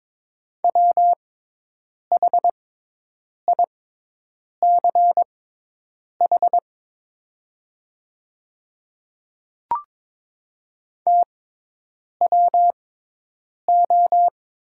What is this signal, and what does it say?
Morse code, heard as a keyed tone, about 22 wpm, spaced as for 8 wpm, 700 Hz. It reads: WHICH TWO
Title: 6x Character Spacing - Top 100 Words - 22wpm